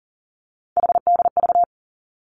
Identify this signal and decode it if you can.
Morse code, heard as a keyed tone, about 40 wpm, 700 Hz. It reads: HDV
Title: Three Challenging Letter-Number Combinations 40wpm